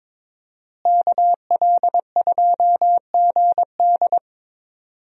Morse code, heard as a keyed tone, about 22 wpm, 700 Hz. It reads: KL2GD
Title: USA Call Signs - 5 Characters - Repeated 2x - 22wpm